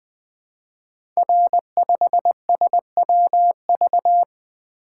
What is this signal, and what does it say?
Morse code, heard as a keyed tone, about 20 wpm, 700 Hz. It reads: R5SWV
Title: Code Groups - 5 Letter:Numbers - 20wpm